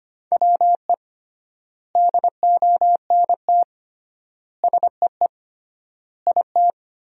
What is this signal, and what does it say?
Morse code, heard as a keyed tone, about 25 wpm, 700 Hz. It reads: WE DONT SEE IT
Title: 3x Word-Spacing - Sentences from Top 300 Words - 25wpm